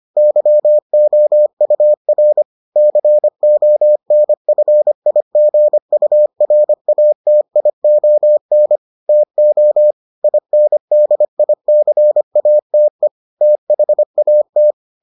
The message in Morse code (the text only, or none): YOUR CONFIGURATION TO INDICATE THAT